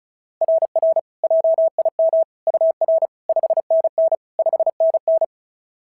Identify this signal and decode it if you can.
Morse code, heard as a keyed tone, about 36 wpm, 690 Hz. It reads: RR JIM UR 5NN 5NN
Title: SOTA QSO Practice 35wpm